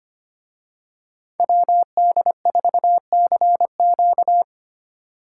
Morse code, heard as a keyed tone, about 29 wpm, 700 Hz. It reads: WD4CQ
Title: Speed Racing - USA 2x2 Call Signs 25wpm